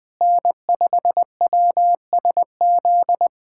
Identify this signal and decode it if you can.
Morse code, heard as a keyed tone, about 21 wpm, 700 Hz. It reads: N5WSZ
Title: USA Callsign DE Callsign - 5 Characters - 20wpm